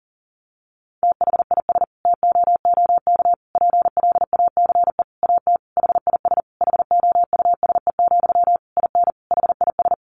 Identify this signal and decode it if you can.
Morse code, heard as a keyed tone, about 40 wpm, 700 Hz. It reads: THIS TOOK PLACE AT HIS HOUSE, IN HIS
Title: Sentences from Top 300 Words 40wpm